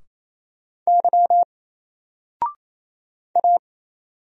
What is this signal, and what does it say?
Morse code, heard as a keyed tone, about 27 wpm, 700 Hz. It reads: Y A
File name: Single-Letter-Number-28wpm